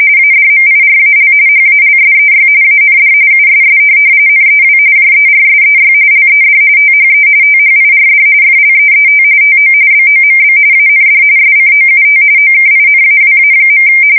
RTTY, about 45.45 baud. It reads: ADMIRAL HALSEY AND USS NEW JERSEY LED THE THIRD FLEET TO BOMB AND SCOUT OKINAWA.
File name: fsk170 / USS-New-Jersey